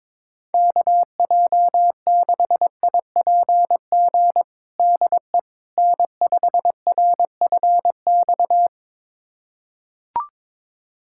Morse code, heard as a keyed tone, about 22 wpm, 700 Hz. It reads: KJ6IPG DE N5RFX
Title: USA Callsign DE Callsign 22wpm